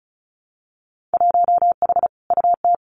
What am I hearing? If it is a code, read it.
Morse code, heard as a keyed tone, about 36 wpm, 700 Hz. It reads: 1H UT
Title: Winter Field Day Exchanges Encore 35wpm